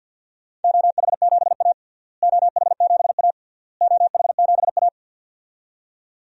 Morse code, heard as a keyed tone, about 52 wpm, 700 Hz. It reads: OH7A OH7A OH7A
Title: DXCC-100 Call Signs - 4 Characters - Repeated 3x - 50wpm